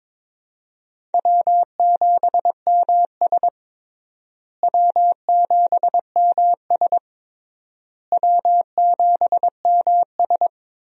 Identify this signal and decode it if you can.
Morse code, heard as a keyed tone, about 21 wpm, 700 Hz. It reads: W7MS W7MS W7MS
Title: USA Call Signs - 4 Characters - Repeated 3x - 22wpm